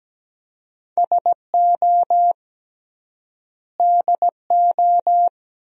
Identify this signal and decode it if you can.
Morse code, heard as a keyed tone, about 17 wpm, 700 Hz. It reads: SO DO
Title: 3x Word-Spacing - Sentences from Top 400 Words - 17wpm